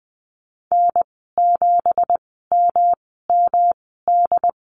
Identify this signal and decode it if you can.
Morse code, heard as a keyed tone, about 20 wpm, spaced as for 15 wpm, 700 Hz. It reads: N7MMD
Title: 186 Lesson 30g - Letter b - Call Signs 20wpm